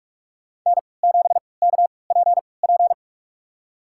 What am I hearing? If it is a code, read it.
Morse code, heard as a keyed tone, about 46 wpm, spaced as for 25 wpm, 700 Hz. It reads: N7KPP
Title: USA Call Signs - 5 Characters - 3x Character Spacing - 45wpm